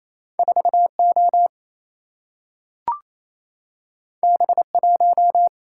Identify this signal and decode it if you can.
Morse code, heard as a keyed tone, about 28 wpm, 700 Hz. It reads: 4 O B 1
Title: Single-Letter Single-Number 28wpm